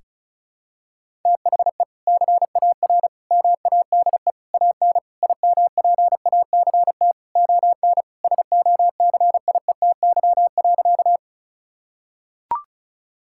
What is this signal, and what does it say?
Morse code, heard as a keyed tone, about 36 wpm, 700 Hz. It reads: THE CAR MADE AN IMPACT ON SOCIETY.
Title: Sentences from Top 1000 Words - Review Part 6 - 35wpm